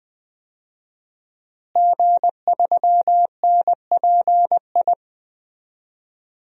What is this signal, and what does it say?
Morse code, heard as a keyed tone, about 20 wpm, 700 Hz. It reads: G3NPI
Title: DXCC-100 Call Signs - 5 Characters - Repeated 2x - 20wpm